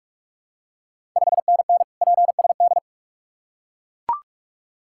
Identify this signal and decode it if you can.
Morse code, heard as a keyed tone, about 44 wpm, 690 Hz. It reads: 5NN PSD
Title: Ontario QSO Party Exchange 45wpm